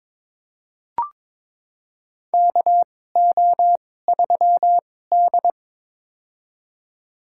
Morse code, heard as a keyed tone, about 22 wpm, spaced as for 16 wpm, 700 Hz. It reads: KO3D
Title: USA Call Signs - 4 Characters - 2x Character Spacing - 22wpm